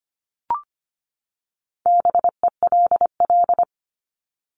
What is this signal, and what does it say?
Morse code, heard as a keyed tone, about 25 wpm, 700 Hz. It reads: BELL
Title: Top 1751-2000 Words 25wpm